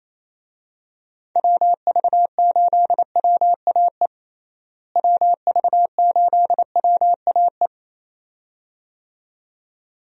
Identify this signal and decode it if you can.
Morse code, heard as a keyed tone, about 28 wpm, 700 Hz. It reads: WV8WAE WV8WAE
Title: USA Call Signs - 6 Characters - Repeated 2x - 28wpm